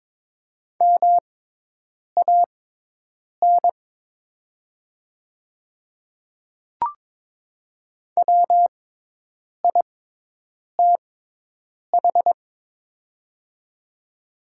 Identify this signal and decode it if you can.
Morse code, heard as a keyed tone, about 22 wpm, spaced as for 8 wpm, 700 Hz. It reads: MAN WITH
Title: 6x Character Spacing - Top 300 Words - 22wpm